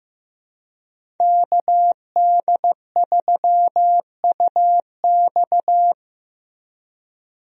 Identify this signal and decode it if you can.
Morse code, heard as a keyed tone, about 16 wpm, 700 Hz. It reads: KD3UX
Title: USA Call Signs - 5 Characters - Repeated 3x - 15wpm